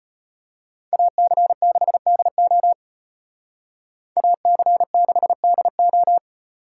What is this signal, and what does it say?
Morse code, heard as a keyed tone, about 32 wpm, 700 Hz. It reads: AC6DO AC6DO